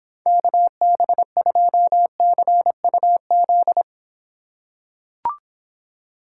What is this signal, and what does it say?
Morse code, heard as a keyed tone, about 26 wpm, 700 Hz. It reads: KB2CUZ